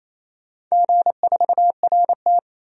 Morse code, heard as a keyed tone, about 28 wpm, 700 Hz. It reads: G4RT
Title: DXCC-100 Call Signs - 4 Characters - 28wpm